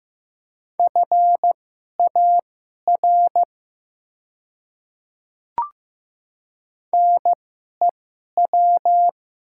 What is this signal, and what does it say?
Morse code, heard as a keyed tone, about 15 wpm, spaced as for 11 wpm, 700 Hz. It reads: FAR NEW
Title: 2x Character Spacing - Top 300 Words - 15wpm